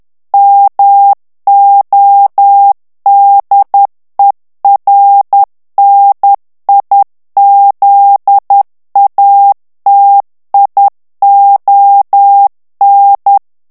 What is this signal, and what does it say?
Morse code, heard as a keyed tone, about 10 wpm, 800 Hz. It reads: MODERNIZATION
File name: ANGLAIS texte clair 3 vitesse 10 mots minute